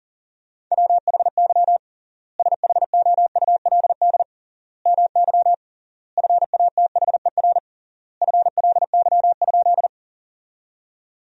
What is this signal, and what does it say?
Morse code, heard as a keyed tone, about 40 wpm, 700 Hz. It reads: WHY SHOULD MY FATHER FLY?
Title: 3x Word-Spacing - Sentences from Top 500 Words - 40wpm